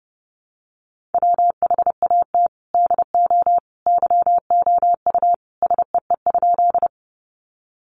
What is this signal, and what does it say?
Morse code, heard as a keyed tone, about 30 wpm, 700 Hz. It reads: WHAT DO YOU SEE?
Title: Sentences from Top 100 Words 30wpm